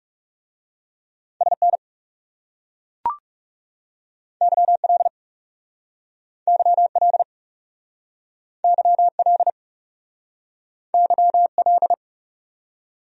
Morse code, heard as a keyed tone, about 38 wpm, 700 Hz. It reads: SN YL YL YL YL